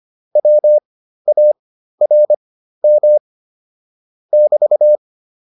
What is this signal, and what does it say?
Morse code, heard as a keyed tone, about 25 wpm, spaced as for 13 wpm, 600 Hz. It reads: WARM <BT>
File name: QSO307 13